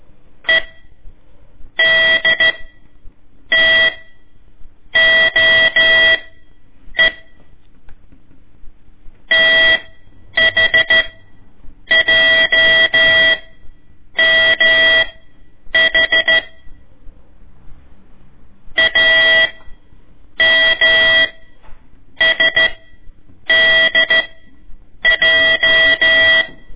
Morse code, heard as a keyed tone, about 13 wpm, spaced as for 7 wpm, 1950 Hz. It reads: EDTOE THJMH AMSDJ
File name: ik1weg2 / CORSI CW / lez4-1